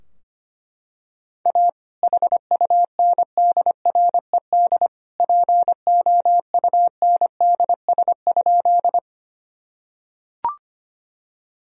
Morse code, heard as a keyed tone, about 25 wpm, 700 Hz. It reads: A HUNDRED POUNDS?
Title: Sentences from Top 1000 Words - Review Part 2 - 25wpm